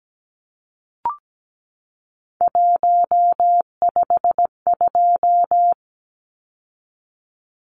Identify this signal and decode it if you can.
Morse code, heard as a keyed tone, about 17 wpm, 700 Hz. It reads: 152
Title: Numbers - 3-Digit 17wpm